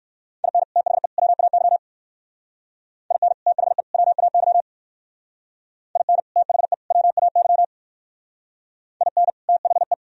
Morse code, heard as a keyed tone, about 57 wpm, 700 Hz. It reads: IN THE WAY IN THE WAY IN THE WAY IN THE